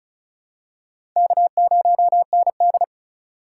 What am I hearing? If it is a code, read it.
Morse code, heard as a keyed tone, about 34 wpm, 700 Hz. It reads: K0ND
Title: USA Call Signs - 4 Characters - 3x Character Spacing - 35wpm